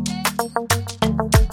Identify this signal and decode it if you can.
There is no code.